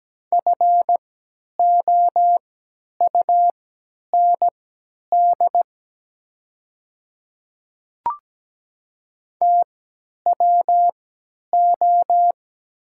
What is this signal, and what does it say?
Morse code, heard as a keyed tone, about 17 wpm, spaced as for 10 wpm, 700 Hz. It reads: FOUND TWO
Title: 3x Character Spacing - Top 300 Words - 17wpm